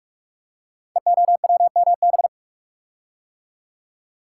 Morse code, heard as a keyed tone, about 44 wpm, 700 Hz. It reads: EOWMB
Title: Code Groups - 5 Letter:Numbers - 45wpm